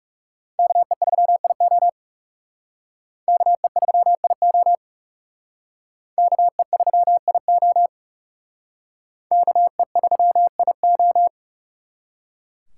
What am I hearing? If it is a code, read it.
Morse code, heard as a keyed tone, about 38 wpm, 700 Hz. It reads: KE3IO KE3IO KE3IO KE3IO